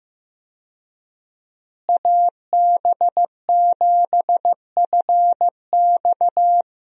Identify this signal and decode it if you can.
Morse code, heard as a keyed tone, about 15 wpm, 700 Hz. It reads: AB7FX
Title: USA Call Signs - 5 Characters - Repeated 3x - 15wpm